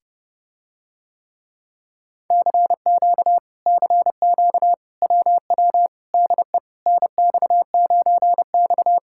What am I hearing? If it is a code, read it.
Morse code, heard as a keyed tone, about 30 wpm, 700 Hz. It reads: CQ CQ WW DE NX9X